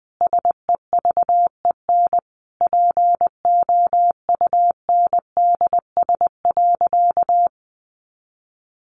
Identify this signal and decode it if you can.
Morse code, heard as a keyed tone, about 20 wpm, 700 Hz. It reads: SEVEN POUNDS.
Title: Easy Phrases 20wpm